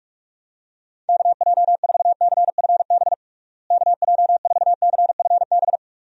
Morse code, heard as a keyed tone, about 41 wpm, 700 Hz. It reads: KJ4CFB KJ4CFB
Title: USA Call Signs - 6 Characters - Repeated 2x - 45wpm